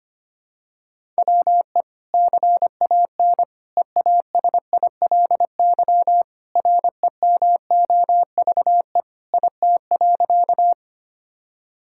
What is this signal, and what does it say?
Morse code, heard as a keyed tone, about 25 wpm, 700 Hz. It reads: WE CAN EASILY REMOVE IT.
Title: Sentences from Top 1000 Words - Review Part 5 - 25wpm